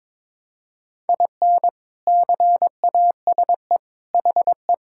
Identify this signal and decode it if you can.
Morse code, heard as a keyed tone, about 22 wpm, 700 Hz. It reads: IN CASE HE
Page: morse-code-ninja-practice-22wpm